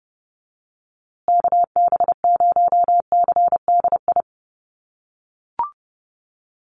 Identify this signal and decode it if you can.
Morse code, heard as a keyed tone, about 30 wpm, 700 Hz. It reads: KB0CDI
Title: USA Call Signs Repeated 2x 30wpm